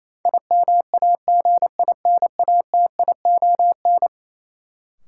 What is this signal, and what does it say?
Morse code, heard as a keyed tone, about 27 wpm, 700 Hz. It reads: IMAGINATION
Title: Sentences from Top 1300 Words - New Word Drill - 28wpm